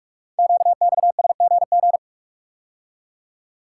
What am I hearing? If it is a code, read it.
Morse code, heard as a keyed tone, about 44 wpm, 700 Hz. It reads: QXSGG
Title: Code Groups - 5 Letter:Numbers - 45wpm